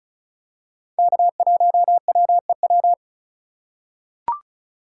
Morse code, heard as a keyed tone, about 34 wpm, 700 Hz. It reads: K1WEW
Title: USA Call Signs - 5 Characters - 3x Character Spacing - 35wpm